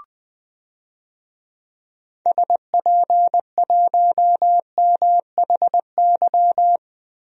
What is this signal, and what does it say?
Morse code, heard as a keyed tone, about 21 wpm, 700 Hz. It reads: SP1MHY
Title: DXCC-100 Call Signs - 6 Characters - Repeated 3x - 20wpm